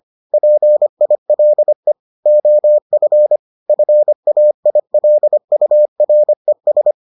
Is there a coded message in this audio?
Yes: Morse code from 0.3 to 6.9 s, heard as a keyed tone, about 25 wpm, 600 Hz. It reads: PILE OF FAILURES